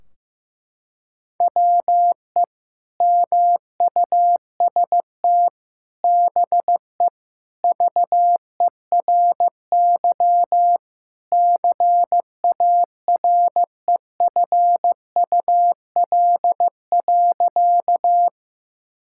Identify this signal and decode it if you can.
Morse code, heard as a keyed tone, about 15 wpm, 700 Hz. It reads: WE MUST BE VERY CAREFUL.
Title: Sentences from Top 1600 Words 15wpm